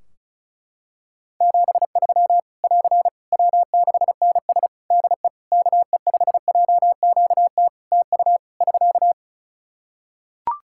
Morse code, heard as a keyed tone, about 36 wpm, 700 Hz. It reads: 73 <AR> W6NS DE KE5JQT TU <SK>
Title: Practice QSO - Exchange 4 - Ending - 35wpm